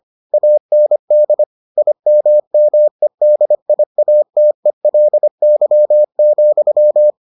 Morse code, heard as a keyed tone, about 25 wpm, 600 Hz. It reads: AND IMMEDIATELY,